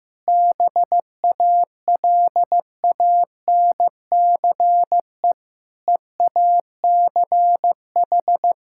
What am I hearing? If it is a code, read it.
Morse code, heard as a keyed tone, about 14 wpm, 700 Hz. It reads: BALANCE EACH
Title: Sentences from Top 1000 Words - Review Part 6 - 15wpm